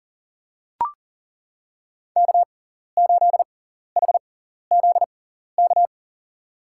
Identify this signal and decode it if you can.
Morse code, heard as a keyed tone, about 37 wpm, spaced as for 14 wpm, 700 Hz. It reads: K8HZK
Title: USA Call Signs - 5 Characters - 6x Character Spacing - 40wpm